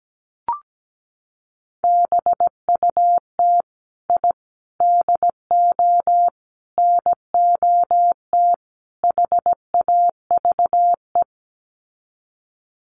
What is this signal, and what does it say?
Morse code, heard as a keyed tone, about 17 wpm, 700 Hz. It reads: BUT I DO NOT HAVE
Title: Sets of 5 Words Encore 17wpm